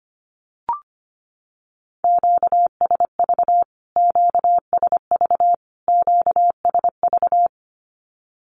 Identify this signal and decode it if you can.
Morse code, heard as a keyed tone, about 25 wpm, 700 Hz. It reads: QSV QSV QSV
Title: Q-Codes Repeated 3x 25wpm